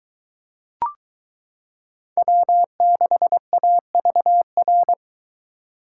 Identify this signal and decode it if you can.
Morse code, heard as a keyed tone, about 23 wpm, 700 Hz. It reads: W6AVR